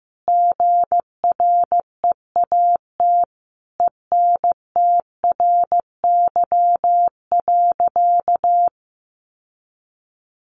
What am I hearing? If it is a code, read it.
Morse code, heard as a keyed tone, about 15 wpm, 700 Hz. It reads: GREAT ENTRY.
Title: Sentences from Top 1500 Words 15wpm